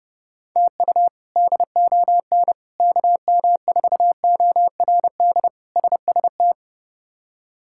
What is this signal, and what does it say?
Morse code, heard as a keyed tone, about 31 wpm, 700 Hz. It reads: TU DON KM4ORD SST
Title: K1USN SST QSO Practice - 30wpm